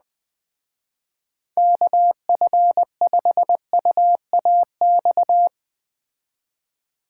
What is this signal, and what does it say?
Morse code, heard as a keyed tone, about 20 wpm, 700 Hz. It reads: KF5UAX